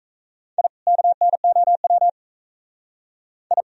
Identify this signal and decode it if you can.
Morse code, heard as a keyed tone, about 40 wpm, 700 Hz. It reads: I KNOW I